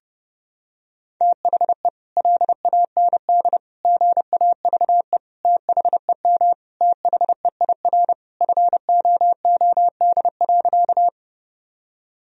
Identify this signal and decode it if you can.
Morse code, heard as a keyed tone, about 30 wpm, 700 Hz. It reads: THE LAND GAVE THEM THEIR FOOD.
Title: Sentences from Top 1000 Words - Review Part 1 - 30wpm